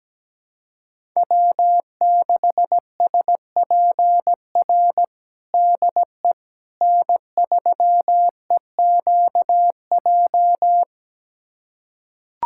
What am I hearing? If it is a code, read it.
Morse code, heard as a keyed tone, about 17 wpm, 700 Hz. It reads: W6SPR DE N3EQJ
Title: USA Callsign DE Callsign - 5 Characters - 17wpm